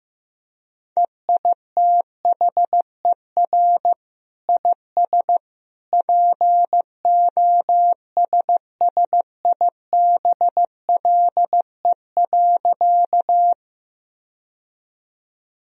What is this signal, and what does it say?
Morse code, heard as a keyed tone, about 15 wpm, 700 Hz. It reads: EITHER IS POSSIBLE.
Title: Sentences from Top 1000 Words - Review Part 4 - 15wpm